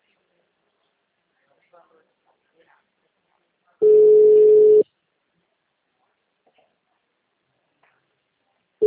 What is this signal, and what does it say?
Ringback tone (Europe, 425 Hz), with speech.
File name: out- -2019- - - .328